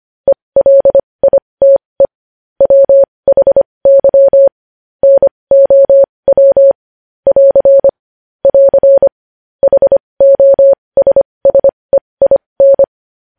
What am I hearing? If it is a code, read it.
Morse code, heard as a keyed tone, about 25 wpm, spaced as for 20 wpm, 550 Hz. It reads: ELITE WHY NOW <AR> <AR> HOSSEIN
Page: News headlines in Morse code at 20 wpm